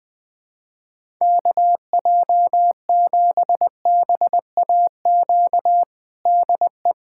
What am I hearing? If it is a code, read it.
Morse code, heard as a keyed tone, about 20 wpm, 700 Hz. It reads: KJ7BAQ DE
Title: USA Callsign DE Callsign 20wpm